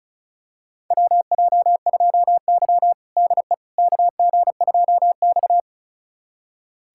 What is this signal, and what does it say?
Morse code, heard as a keyed tone, about 36 wpm, 700 Hz. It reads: WJ2Y DE KG2X
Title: USA Callsign DE Callsign - 4 Characters - 35wpm